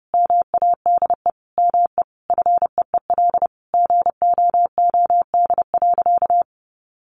Morse code, heard as a keyed tone, about 30 wpm, 700 Hz. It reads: MADE ME FEEL GOOD.
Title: Sentences from Top 600 Words 30wpm